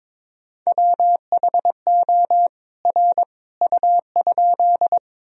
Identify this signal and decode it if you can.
Morse code, heard as a keyed tone, about 22 wpm, 700 Hz. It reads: WHO R U?
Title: Sentences from Top 100 Words and Abbreviations 22wpm